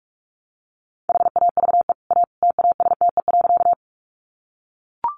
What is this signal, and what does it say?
Morse code, heard as a keyed tone, about 44 wpm, 700 Hz. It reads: HAVE A TASTE.
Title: Sentences from Top 1100 Words 45wpm